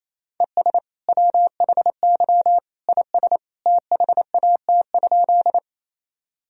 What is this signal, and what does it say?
Morse code, heard as a keyed tone, about 28 wpm, 700 Hz. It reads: ES WHY IS THAT?
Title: Sentences from Top 200 Words and Abbreviations 28wpm